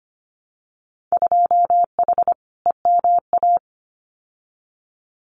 Morse code, heard as a keyed tone, about 25 wpm, 700 Hz. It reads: 2H EMA
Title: Winter Field Day Exchanges 25wpm